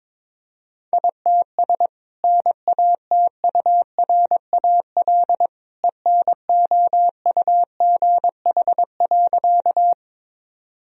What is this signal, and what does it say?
Morse code, heard as a keyed tone, about 22 wpm, 700 Hz. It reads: ITS NATURAL ENOUGH.
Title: Sentences from Top 1000 Words - Review Part 4 - 22wpm